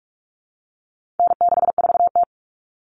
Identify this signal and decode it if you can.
Morse code, heard as a keyed tone, about 44 wpm, 700 Hz. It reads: N64T
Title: Common Cut Numbers - 4-Digit 45wpm